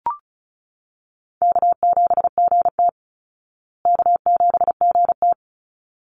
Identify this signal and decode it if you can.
Morse code, heard as a keyed tone, about 34 wpm, 700 Hz. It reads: K7GT K7GT
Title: Easy Call Signs Repeated 2x - 35wpm